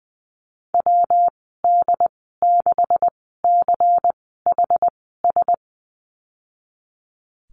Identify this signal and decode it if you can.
Morse code, heard as a keyed tone, about 20 wpm, spaced as for 15 wpm, 700 Hz. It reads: WD6CHS
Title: 147 Lesson 24g - Letter y - Call Signs 20wpm